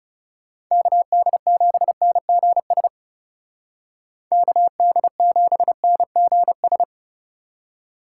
Morse code, heard as a keyed tone, about 39 wpm, 700 Hz. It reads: KD7NGS KD7NGS